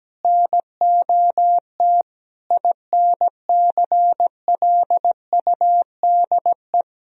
Morse code, heard as a keyed tone, about 17 wpm, 700 Hz. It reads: NOT INCLUDE
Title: Sentences from Top 1000 Words - Review Part 3 - 17wpm